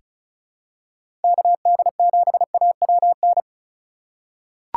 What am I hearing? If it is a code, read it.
Morse code, heard as a keyed tone, about 34 wpm, 700 Hz. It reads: KD7AWN
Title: USA Call Signs - 6 Characters - 2x Character Spacing - 35wpm